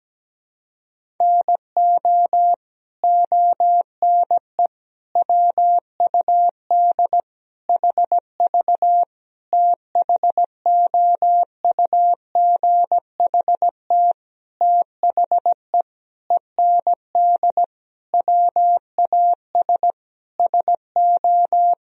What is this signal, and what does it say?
Morse code, heard as a keyed tone, about 17 wpm, 700 Hz. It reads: NO ONE WUD HV THOUGHT THE END WAS SO
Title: Sentences from Top 300 Words and Abbreviations 17wpm